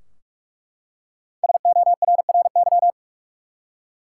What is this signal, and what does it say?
Morse code, heard as a keyed tone, about 42 wpm, 700 Hz. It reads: SORRY
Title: Five Letter Words - 45wpm